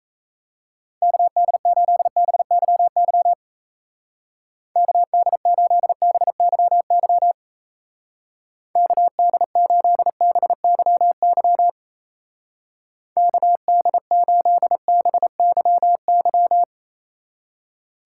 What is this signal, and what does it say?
Morse code, heard as a keyed tone, about 37 wpm, 700 Hz. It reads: KD8BYY KD8BYY KD8BYY KD8BYY